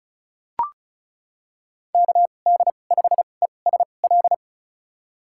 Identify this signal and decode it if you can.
Morse code, heard as a keyed tone, about 34 wpm, spaced as for 25 wpm, 700 Hz. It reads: KD5ESL
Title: USA Call Signs - 6 Characters - 2x Character Spacing - 35wpm